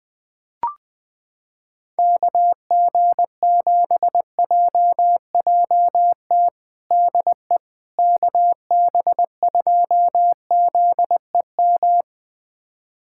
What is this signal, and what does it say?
Morse code, heard as a keyed tone, about 20 wpm, 700 Hz. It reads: KG7JJT DE KB2ZEM
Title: USA Callsign DE Callsign - 6 Characters - 20wpm